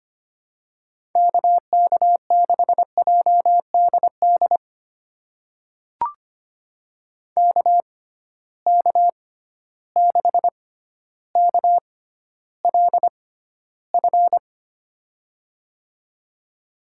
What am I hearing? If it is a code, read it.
Morse code, heard as a keyed tone, about 25 wpm, spaced as for 13 wpm, 700 Hz. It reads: KK6JDD KK6KLF
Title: USA Call Signs - 6 Characters - 6x Character Spacing - 25wpm